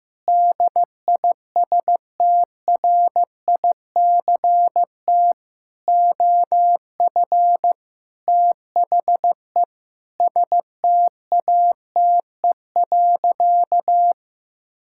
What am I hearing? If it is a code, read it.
Morse code, heard as a keyed tone, about 15 wpm, 700 Hz. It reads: DISTRICT OF THE STATE.
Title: Sentences from Top 1600 Words 15wpm